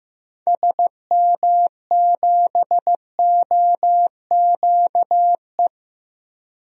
Morse code, heard as a keyed tone, about 15 wpm, 700 Hz. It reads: SM7OQE
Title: DXCC-100 Call Signs - 6 Characters - 15wpm